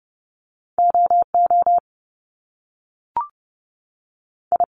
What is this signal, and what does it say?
Morse code, heard as a keyed tone, about 30 wpm, 700 Hz. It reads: O O I
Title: Top Two Letter Combinations in Words 30wpm